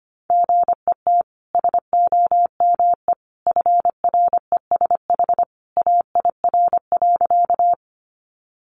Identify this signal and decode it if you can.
Morse code, heard as a keyed tone, about 25 wpm, 700 Hz. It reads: GET SOME FRESH AIR.
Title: Sentences from Top 1200 Words 25wpm